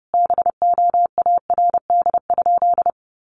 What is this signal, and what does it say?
Morse code, heard as a keyed tone, about 30 wpm, 700 Hz. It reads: BOARD?
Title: Sentences from Top 1300 Words 30wpm